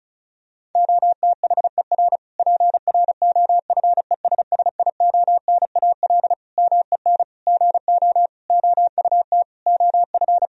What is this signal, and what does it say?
Morse code, heard as a keyed tone, about 36 wpm, 700 Hz. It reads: OTHER PROFESSIONAL MEN GO OUT OF
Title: Sentences from Top 1000 Words - Review Part 4 - 35wpm